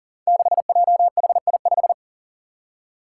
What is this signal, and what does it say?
Morse code, heard as a keyed tone, about 39 wpm, 700 Hz. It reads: 6JHI5
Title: Five Challenging Letter-Number Combinations 40wpm